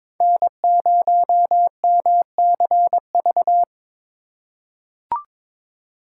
Morse code, heard as a keyed tone, about 22 wpm, 700 Hz. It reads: N0MCV
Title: USA Call Signs - 5 Characters - 3x Character Spacing - 22wpm